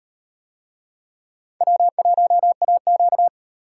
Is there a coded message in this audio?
Yes: Morse code from 1.6 to 3.3 s, heard as a keyed tone, about 38 wpm, 700 Hz. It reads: W1AQ